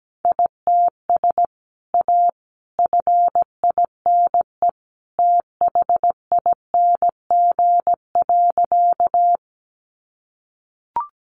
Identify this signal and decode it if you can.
Morse code, heard as a keyed tone, about 17 wpm, 700 Hz. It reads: ITS A FINE THING.
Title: Sentences from Top 600 Words 17wpm